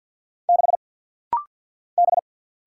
Morse code, heard as a keyed tone, about 49 wpm, 700 Hz. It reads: 6 B
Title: Single Letter-Number - Rapid-Fire 50wpm